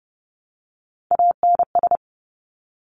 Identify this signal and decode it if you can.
Morse code, heard as a keyed tone, about 30 wpm, 700 Hz. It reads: ANS
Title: Top Three Letter Combinations in Words 30wpm